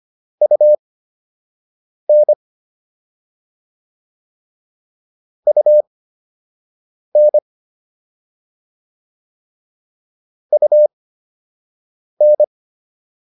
Morse code, heard as a keyed tone, about 25 wpm, spaced as for 6 wpm, 600 Hz. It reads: UN UN UN
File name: UN7 06